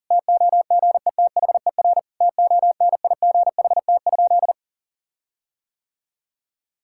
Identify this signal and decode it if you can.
Morse code, heard as a keyed tone, about 37 wpm, 700 Hz. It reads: TOGETHER TONIGHT?
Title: Sentences from Top 1600 Words 40wpm